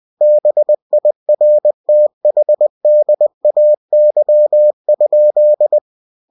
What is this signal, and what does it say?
Morse code, heard as a keyed tone, about 20 wpm, 600 Hz. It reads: BIRTHDAY?